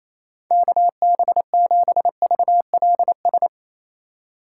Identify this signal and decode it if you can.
Morse code, heard as a keyed tone, about 28 wpm, 700 Hz. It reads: KB7VLS
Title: USA Call Signs - 6 Characters - 2x Character Spacing - 28wpm